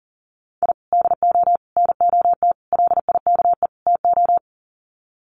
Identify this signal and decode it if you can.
Morse code, heard as a keyed tone, about 40 wpm, 700 Hz. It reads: I DO NOT LIKE TO
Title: Sets of 5 Words 40wpm